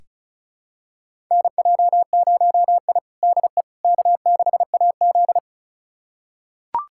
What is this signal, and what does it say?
Morse code, heard as a keyed tone, about 34 wpm, 700 Hz. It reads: NJ0I DE K6AZ
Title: USA Callsign DE Callsign - 4 Characters - 35wpm